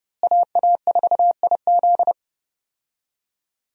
Morse code, heard as a keyed tone, about 32 wpm, 700 Hz. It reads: AA4IZ
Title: USA Call Signs - 5 Characters - Repeated 3x - 30wpm